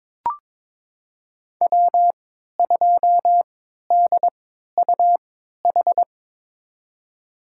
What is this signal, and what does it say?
Morse code, heard as a keyed tone, about 22 wpm, spaced as for 13 wpm, 700 Hz. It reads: W2DUH
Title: USA Call Signs - 5 Characters - 3x Character Spacing - 22wpm